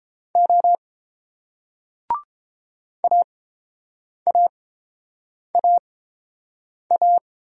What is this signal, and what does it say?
Morse code, heard as a keyed tone, about 30 wpm, spaced as for 7 wpm, 700 Hz. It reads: O AAAA